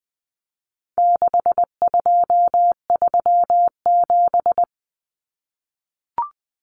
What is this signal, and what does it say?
Morse code, heard as a keyed tone, about 20 wpm, 700 Hz. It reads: 6237
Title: Numbers - 4-Digit 20wpm